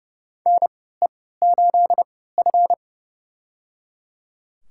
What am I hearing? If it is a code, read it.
Morse code, heard as a keyed tone, about 30 wpm, spaced as for 17 wpm, 700 Hz. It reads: NE8F
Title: USA Call Signs - 4 Characters - 3x Character Spacing - 30wpm